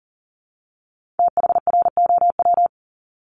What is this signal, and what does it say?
Morse code, heard as a keyed tone, about 42 wpm, 700 Hz. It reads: THROW
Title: Multi-letter Phonemes 40wpm